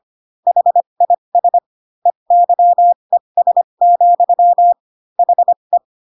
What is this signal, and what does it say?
Morse code, heard as a keyed tone, about 25 wpm, spaced as for 22 wpm, 700 Hz. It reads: HIS EYES, HE